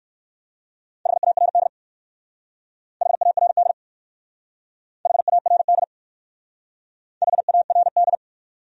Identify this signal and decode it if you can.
Morse code, heard as a keyed tone, about 59 wpm, 690 Hz. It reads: HARD HARD HARD HARD